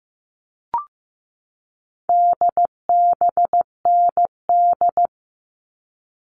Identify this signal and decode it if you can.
Morse code, heard as a keyed tone, about 15 wpm, 700 Hz. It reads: DBND
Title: All Cut Numbers - 4-Digit 15wpm